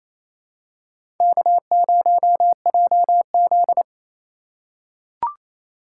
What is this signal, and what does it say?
Morse code, heard as a keyed tone, about 28 wpm, 700 Hz. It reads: K0JZ
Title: USA Call Signs - 4 Characters - 3x Character Spacing - 28wpm